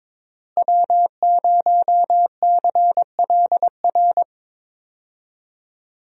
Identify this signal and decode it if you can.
Morse code, heard as a keyed tone, about 21 wpm, 700 Hz. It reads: W0CLR
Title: USA Call Signs - 5 Characters - Repeated 3x - 22wpm